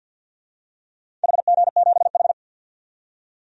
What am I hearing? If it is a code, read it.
Morse code, heard as a keyed tone, about 47 wpm, 700 Hz. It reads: HG7H
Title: DXCC-100 Call Signs - 4 Characters - 50wpm